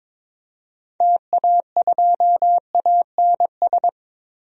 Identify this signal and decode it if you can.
Morse code, heard as a keyed tone, about 22 wpm, 700 Hz. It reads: TA2ANS
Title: DXCC-100 Call Signs - 6 Characters - Repeated 3x - 22wpm